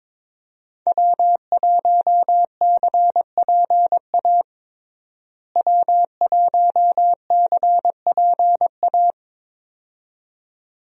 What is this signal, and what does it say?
Morse code, heard as a keyed tone, about 22 wpm, 700 Hz. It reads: W1CPA W1CPA
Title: USA Call Signs - 5 Characters - Repeated 2x - 22wpm